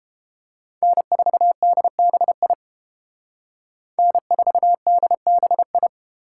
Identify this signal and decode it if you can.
Morse code, heard as a keyed tone, about 29 wpm, 700 Hz. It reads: N4DBI N4DBI